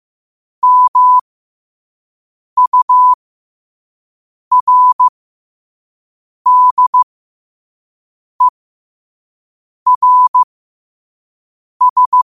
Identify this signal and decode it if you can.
Morse code, heard as a keyed tone, about 15 wpm, spaced as for 5 wpm, 1000 Hz. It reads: MURDERS
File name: qotdfiles / 5wpm